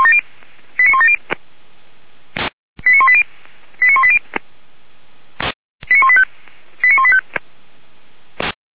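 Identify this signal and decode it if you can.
Five-tone selective call (ZVEI), heard as quick runs of five tones, tones of about 80 ms. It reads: 89170 89180 89180 89185 89185
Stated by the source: ZVEI five-tone call 89138 received by SDR, logged by feuerwehr-orlamuende.org fire brigade server.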